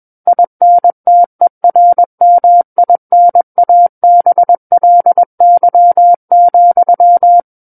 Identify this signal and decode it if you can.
Morse code, heard as a keyed tone, about 21 wpm, 700 Hz. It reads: INTERMINABLY,
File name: Book One / 12-Chapter 12 - 21 WPM